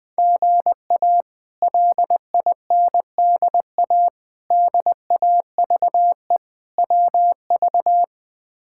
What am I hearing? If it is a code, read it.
Morse code, heard as a keyed tone, about 20 wpm, 700 Hz. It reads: GA LINDA DAVE WV
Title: K1USN SST QSO Practice - 20wpm